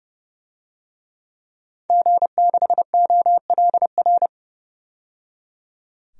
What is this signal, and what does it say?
Morse code, heard as a keyed tone, about 30 wpm, 700 Hz. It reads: G6OLR
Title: DXCC-100 Call Signs - 5 Characters - 30wpm